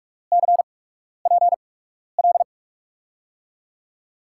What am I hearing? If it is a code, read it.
Morse code, heard as a keyed tone, about 44 wpm, spaced as for 12 wpm, 700 Hz. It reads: CPL
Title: USA Call Signs - 8x Character Spacing - 45wpm